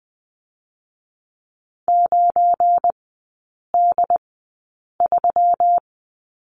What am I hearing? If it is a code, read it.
Morse code, heard as a keyed tone, about 20 wpm, spaced as for 8 wpm, 700 Hz. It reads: 9D3
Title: 178 Lesson 29e - Slash - Three Characters 20wpm